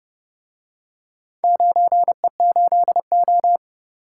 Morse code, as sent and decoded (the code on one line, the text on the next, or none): ----. ---.. ---
9E8O